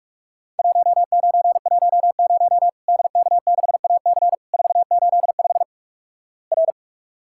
Morse code, heard as a keyed tone, about 46 wpm, 700 Hz. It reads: 1910 DK6AC 485 R
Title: Worked All Europe DX Contest - QTC Exchange - 45wpm